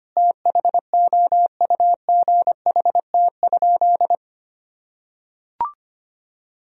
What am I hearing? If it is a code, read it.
Morse code, heard as a keyed tone, about 25 wpm, 700 Hz. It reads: THOUGHT?
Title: Sentences from Top 300 Words and Abbreviations 25wpm